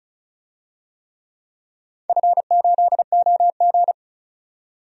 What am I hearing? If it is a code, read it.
Morse code, heard as a keyed tone, about 34 wpm, 700 Hz. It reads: F8OG
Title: DXCC-100 Call Signs - 4 Characters - 35wpm